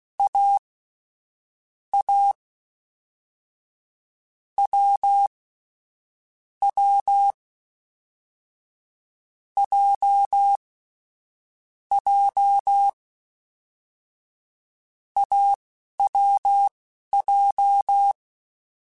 Morse code, heard as a keyed tone, about 16 wpm, spaced as for 11 wpm, 800 Hz. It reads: A A W W J J AWJ
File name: Short Runs / Short Runs-mp3s / awj1ndb6gz7 @10